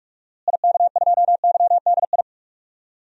Morse code, heard as a keyed tone, about 43 wpm, 700 Hz. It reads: IK2YDI
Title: DXCC-100 Call Signs - 6 Characters - Repeated 3x - 45wpm